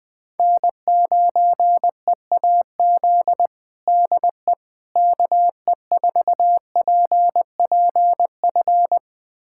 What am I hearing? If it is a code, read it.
Morse code, heard as a keyed tone, about 20 wpm, 700 Hz. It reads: N9EAZ DE KE4PPF
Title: USA Callsign DE Callsign 20wpm